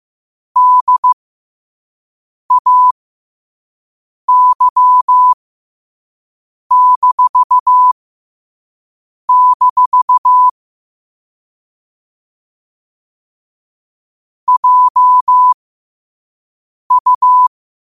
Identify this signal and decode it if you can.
Morse code, heard as a keyed tone, about 15 wpm, spaced as for 5 wpm, 1000 Hz. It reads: DAY-- JU
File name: qotdfiles / 5wpm